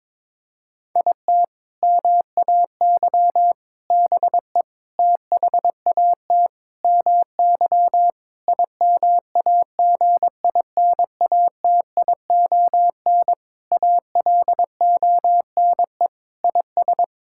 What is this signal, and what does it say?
Morse code, heard as a keyed tone, about 22 wpm, 700 Hz. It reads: IT MAY BE THAT MY IMAGINATION ALONE IS
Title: Sentences from Top 1600 Words 22wpm